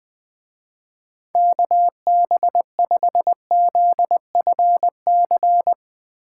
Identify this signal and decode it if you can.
Morse code, heard as a keyed tone, about 21 wpm, 700 Hz. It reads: KB5ZFC
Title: Speed Racing - USA 2x3 Call Signs 17wpm